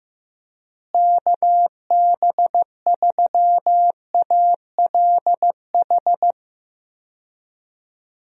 Morse code, heard as a keyed tone, about 15 wpm, 700 Hz. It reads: KB3ALH
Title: USA Call Signs - 6 Characters - Repeated 3x - 15wpm